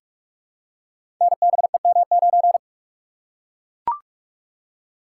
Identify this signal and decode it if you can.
Morse code, heard as a keyed tone, about 44 wpm, 690 Hz. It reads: NBEM9
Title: Code Groups - 5 Letter:Numbers - 45wpm